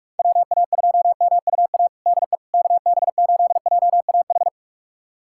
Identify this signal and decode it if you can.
Morse code, heard as a keyed tone, about 46 wpm, 700 Hz. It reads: WA2MUA DE KB8JAH
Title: USA Callsign DE Callsign 45wpm